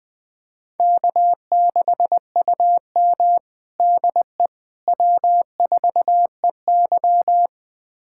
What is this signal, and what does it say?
Morse code, heard as a keyed tone, about 21 wpm, 700 Hz. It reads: K6UM DE W4EY
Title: USA Callsign DE Callsign - 4 Characters - 20wpm